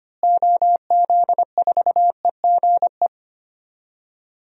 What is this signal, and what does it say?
Morse code, heard as a keyed tone, about 25 wpm, 700 Hz. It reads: OZ4EGE
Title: DXCC-100 Call Signs - 6 Characters - Repeated 3x - 25wpm